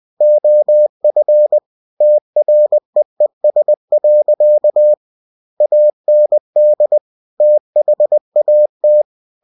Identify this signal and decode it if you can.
Morse code, heard as a keyed tone, about 20 wpm, 600 Hz. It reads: OF TREES. AND THAT
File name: cw-books / winnie-the-pooh / winnie-20wpm-page-0003